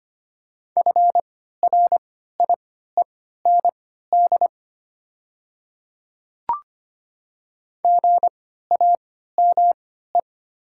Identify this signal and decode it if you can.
Morse code, heard as a keyed tone, about 25 wpm, spaced as for 14 wpm, 700 Hz. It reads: FRIEND GAME
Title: 3x Character Spacing - Top 500 Words - 25wpm